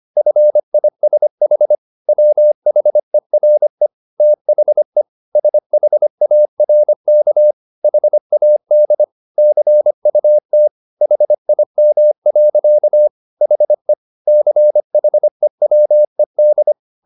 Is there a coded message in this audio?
Yes: Morse code from 0.2 to 16.7 s, heard as a keyed tone, about 25 wpm, 600 Hz. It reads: FISH WHERE THE SHARK HAD CUT HIM. HE CHEWED